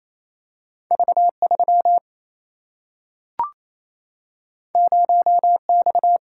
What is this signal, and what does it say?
Morse code, heard as a keyed tone, about 28 wpm, 700 Hz. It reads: V3 0X
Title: Single-Letter Single-Number 28wpm